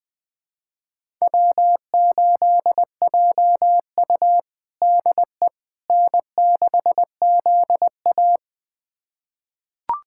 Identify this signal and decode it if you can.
Morse code, heard as a keyed tone, about 20 wpm, 700 Hz. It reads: W8JU DE N6ZA
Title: USA Callsign DE Callsign - 4 Characters - 20wpm